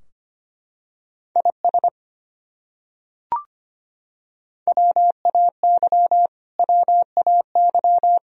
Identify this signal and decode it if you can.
Morse code, heard as a keyed tone, about 25 wpm, 700 Hz. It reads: IS WAY WAY
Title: Top 100 Words Repeated 25wpm